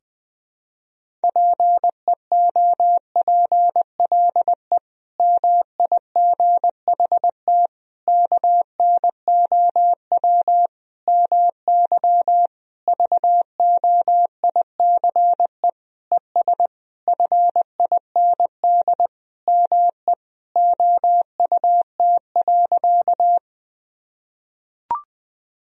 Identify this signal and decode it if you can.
Morse code, heard as a keyed tone, about 20 wpm, 700 Hz. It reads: PEOPLE MIGHT KNOW MY VOICE ES FIND ME OUT.
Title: Sentences from Top 500 Words and Abbreviations 20wpm